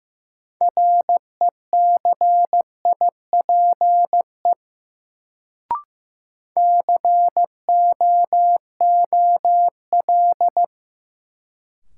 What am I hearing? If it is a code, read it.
Morse code, heard as a keyed tone, about 15 wpm, 700 Hz. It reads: RECIPE COOL
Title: Sentences from Top 1200 Words - New Word Drill - 15wpm